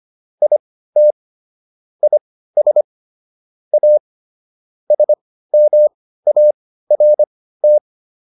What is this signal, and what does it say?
Morse code, heard as a keyed tone, about 25 wpm, spaced as for 15 wpm, 600 Hz. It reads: IT IS A SMART